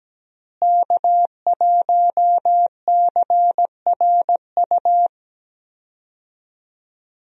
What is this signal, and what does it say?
Morse code, heard as a keyed tone, about 17 wpm, 700 Hz. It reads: K1CRU
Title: USA Call Signs - 5 Characters - Repeated 3x - 17wpm